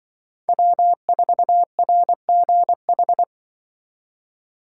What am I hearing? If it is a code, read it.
Morse code, heard as a keyed tone, about 25 wpm, 700 Hz. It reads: W4RGH
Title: Speed Racing - USA 1x3 Call Signs 20wpm